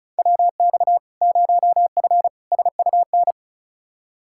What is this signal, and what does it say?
Morse code, heard as a keyed tone, about 36 wpm, 700 Hz. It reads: WX 0F SUN
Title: Practice QSO - Exchange 2 - Info - 35wpm